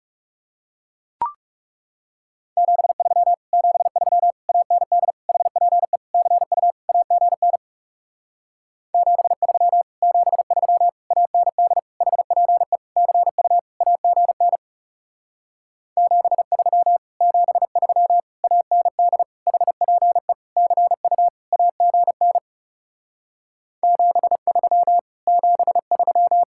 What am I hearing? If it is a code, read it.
Morse code, heard as a keyed tone, about 39 wpm, 700 Hz. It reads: 73 73 AND HPE CU AGN 73 73 AND HPE CU AGN 73 73 AND HPE CU AGN 73 73